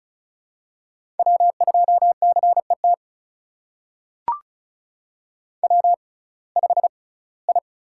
Morse code, heard as a keyed tone, about 36 wpm, spaced as for 21 wpm, 700 Hz. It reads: W2CET W5I
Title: USA Call Signs - 5 Characters - 6x Character Spacing - 35wpm